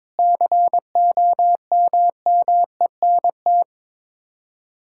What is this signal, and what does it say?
Morse code, heard as a keyed tone, about 21 wpm, 700 Hz. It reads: COMMENT
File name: practice / 22wpm / Sets of 7 Words - 22wpm